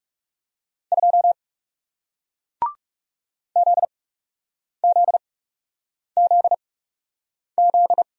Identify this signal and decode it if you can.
Morse code, heard as a keyed tone, about 40 wpm, spaced as for 8 wpm, 700 Hz. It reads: 2 ZZZZ